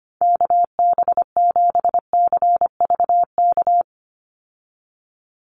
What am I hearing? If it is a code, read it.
Morse code, heard as a keyed tone, about 25 wpm, 700 Hz. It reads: KB7CVK
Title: USA Call Signs 25wpm